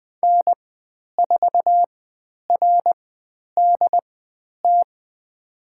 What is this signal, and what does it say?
Morse code, heard as a keyed tone, about 20 wpm, spaced as for 10 wpm, 700 Hz. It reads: N4RDT
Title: 102 Lesson 17g - Letter u - Call Signs 20wpm